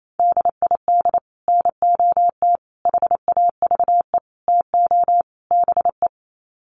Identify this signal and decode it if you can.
Morse code, heard as a keyed tone, about 28 wpm, 700 Hz. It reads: DID NOT HAVE TO BE
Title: Sets of 5 Words Encore2 28wpm